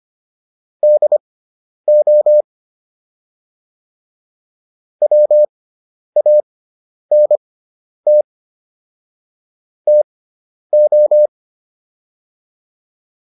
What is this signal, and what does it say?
Morse code, heard as a keyed tone, about 25 wpm, spaced as for 10 wpm, 600 Hz. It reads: DO WANT TO